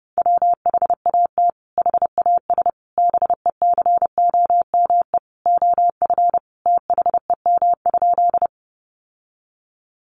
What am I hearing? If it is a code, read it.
Morse code, heard as a keyed tone, about 30 wpm, 700 Hz. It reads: WHAT HAS BECOME OF THEM?
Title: Sentences from Top 600 Words 30wpm